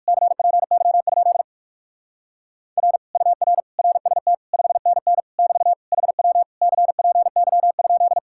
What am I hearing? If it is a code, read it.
Morse code, heard as a keyed tone, about 57 wpm, 700 Hz. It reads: CPY? R UR RST 5NN <BT> HW CPY?